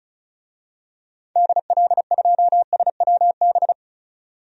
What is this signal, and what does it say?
Morse code, heard as a keyed tone, about 36 wpm, 700 Hz. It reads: DL2SWB
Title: DXCC-100 Call Signs - 6 Characters - 35wpm